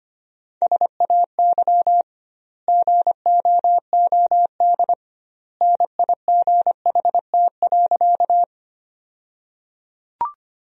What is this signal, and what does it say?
Morse code, heard as a keyed tone, about 25 wpm, 700 Hz. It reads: SAY GOOD NIGHT.
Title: 2x Word-Spacing - Sentences from Top 300 Words - 25wpm